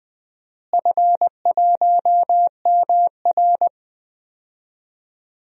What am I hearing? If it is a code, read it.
Morse code, heard as a keyed tone, about 20 wpm, 700 Hz. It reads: F1MR
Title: DXCC-100 Call Signs - 4 Characters - Repeated 2x - 20wpm